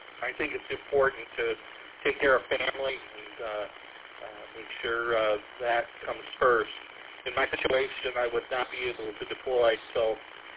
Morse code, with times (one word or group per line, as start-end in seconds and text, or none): none